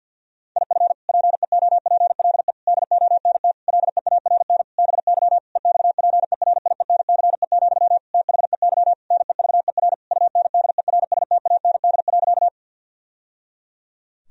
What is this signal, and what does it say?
Morse code, heard as a keyed tone, about 47 wpm, 700 Hz. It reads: IF PEOPLE DONT LEARN BY EXPERIENCE, THEY NEVER UNDERSTAND.
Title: Sentences from Top 1000 Words - Review Part 3 - 50wpm